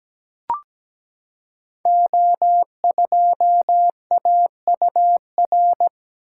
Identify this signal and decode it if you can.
Morse code, heard as a keyed tone, about 17 wpm, 700 Hz. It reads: O2AUR
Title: Code Groups - 5 Letter:Numbers - 17wpm